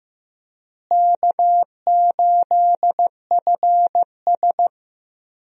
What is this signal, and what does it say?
Morse code, heard as a keyed tone, about 15 wpm, 700 Hz. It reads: K8FS
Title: USA Call Signs - 4 Characters - Repeated 3x - 15wpm